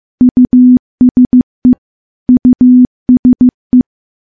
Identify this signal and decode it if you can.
Morse code, heard as a keyed tone, about 15 wpm, 250 Hz. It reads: USE USE
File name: morse-training-15WPM